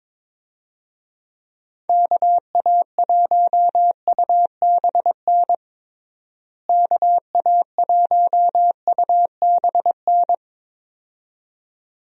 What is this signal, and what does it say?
Morse code, heard as a keyed tone, about 22 wpm, 700 Hz. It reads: KA1UBN KA1UBN
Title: USA Call Signs - 6 Characters - Repeated 2x - 22wpm